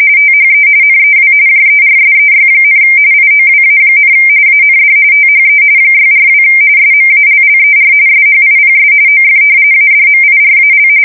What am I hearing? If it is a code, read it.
RTTY, about 45.45 baud. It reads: THE NEW JERSEY, IOWA, AND THE 642 SMALLER SHIPS OF ADMIRAL
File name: fsk170 / battleshipnj